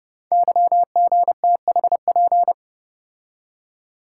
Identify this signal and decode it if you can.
Morse code, heard as a keyed tone, about 28 wpm, 700 Hz. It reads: YGTHP
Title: Code Groups - 5 Letters - 30wpm